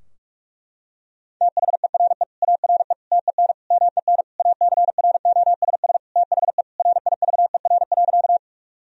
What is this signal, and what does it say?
Morse code, heard as a keyed tone, about 41 wpm, 700 Hz. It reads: THERE ARE TEN MEN ACROSS THE RIVER.
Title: Sentences from Top 1000 Words - Review Part 2 - 45wpm